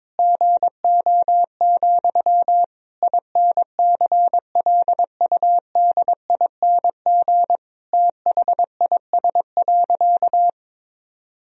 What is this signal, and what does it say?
Morse code, heard as a keyed tone, about 22 wpm, 700 Hz. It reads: GO, INCLUDING THIS.
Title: Sentences from Top 1000 Words - Review Part 4 - 22wpm